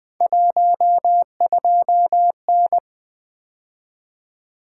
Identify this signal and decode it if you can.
Morse code, heard as a keyed tone, about 20 wpm, 700 Hz. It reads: 12N
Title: Three Letter-Number Combinations - 20wpm